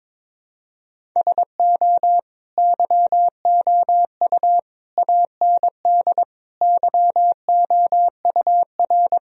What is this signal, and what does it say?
Morse code, heard as a keyed tone, about 22 wpm, 700 Hz. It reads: SO YOU AND YOUR
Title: Sentences from Top 1000 Words - Review Part 4 - 22wpm